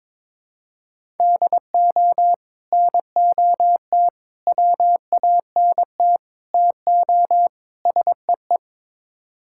Morse code, heard as a keyed tone, about 22 wpm, 700 Hz. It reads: DO NOT WANT TO SEE
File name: practice / 22wpm / Build A Phrase - 5-Word Phrases - 22wpm